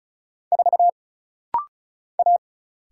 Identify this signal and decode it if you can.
Morse code, heard as a keyed tone, about 34 wpm, 700 Hz. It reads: 4 A
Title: Single Letter-Number Alt - Rapid-Fire 35wpm